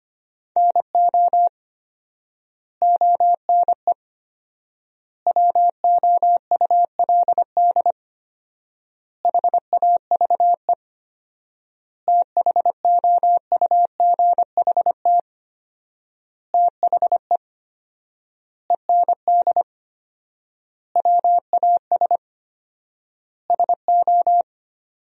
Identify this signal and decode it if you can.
Morse code, heard as a keyed tone, about 25 wpm, 700 Hz. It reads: NO ONE WOULD HAVE THOUGHT THE END WAS SO
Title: 4x Word-Spacing - Sentences from Top 300 Words - 25wpm